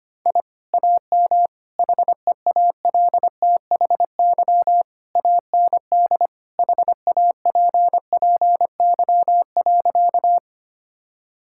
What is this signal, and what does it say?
Morse code, heard as a keyed tone, about 25 wpm, 700 Hz. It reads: I AM HEALTHY AND HAPPY.
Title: Sentences from Top 1000 Words - Review Part 5 - 25wpm